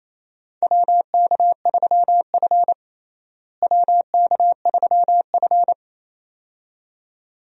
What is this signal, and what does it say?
Morse code, heard as a keyed tone, about 28 wpm, 700 Hz. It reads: WK3F WK3F
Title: USA Call Signs - 4 Characters - Repeated 2x - 28wpm